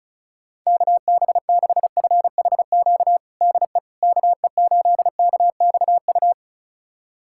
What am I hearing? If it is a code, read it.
Morse code, heard as a keyed tone, about 34 wpm, 700 Hz. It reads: KB6FHQ DE KE8KXU
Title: USA Callsign DE Callsign 35wpm